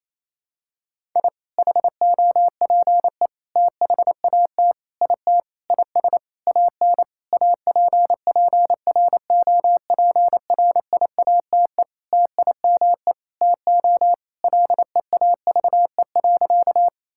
Morse code, heard as a keyed tone, about 28 wpm, 700 Hz. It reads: I HOPE THAT IT IS AN APPROPRIATE TIME TO LEAVE.
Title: Sentences from Top 1000 Words - Review Part 5 - 28wpm